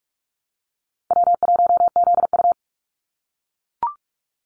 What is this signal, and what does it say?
Morse code, heard as a keyed tone, about 44 wpm, 700 Hz. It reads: W1ZU
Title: NA SOTA Activator Callsigns 45wpm